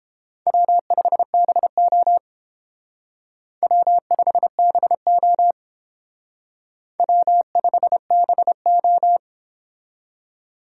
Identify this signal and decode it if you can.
Morse code, heard as a keyed tone, about 29 wpm, 700 Hz. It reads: W5BO W5BO W5BO